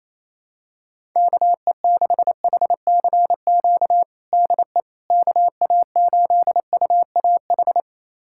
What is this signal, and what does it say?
Morse code, heard as a keyed tone, about 28 wpm, 700 Hz. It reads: KE6HCQ DE KA8UAH
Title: USA Callsign DE Callsign 28wpm